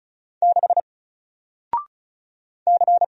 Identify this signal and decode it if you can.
Morse code, heard as a keyed tone, about 36 wpm, 700 Hz. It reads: 6 C
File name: Single-Letter-Number-35wpm